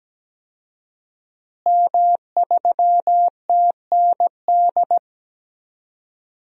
Morse code, heard as a keyed tone, about 17 wpm, 700 Hz. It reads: M3TND
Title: DXCC-100 Call Signs - 5 Characters - Repeated 2x - 17wpm